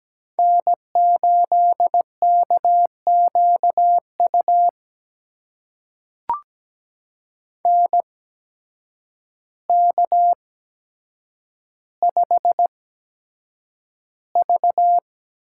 Morse code, heard as a keyed tone, about 17 wpm, spaced as for 8 wpm, 700 Hz. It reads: N8KQU NK5V
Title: USA Call Signs - 8x Character Spacing - 17wpm